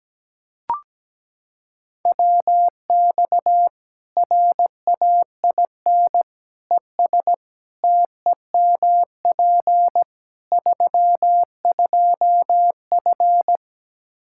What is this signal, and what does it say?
Morse code, heard as a keyed tone, about 17 wpm, 700 Hz. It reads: WX RAIN ES TEMP 32F
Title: K1USN SST QSO Practice 17wpm